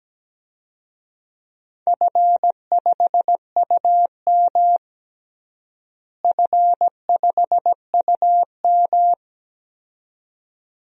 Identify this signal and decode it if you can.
Morse code, heard as a keyed tone, about 17 wpm, 700 Hz. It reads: F5UM F5UM
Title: DXCC-100 Call Signs - 4 Characters - Repeated 2x - 17wpm